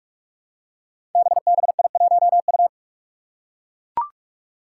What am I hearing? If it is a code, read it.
Morse code, heard as a keyed tone, about 44 wpm, 700 Hz. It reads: BBI1U
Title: Five Challenging Letter-Number Combinations 45wpm